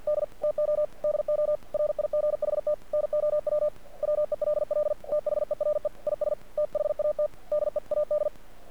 Morse code, heard as a keyed tone, about 48 wpm, 600 Hz. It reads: D TO DO RIGHT NOW WELL THERE IS THAT DEAD
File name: archived / CRSnet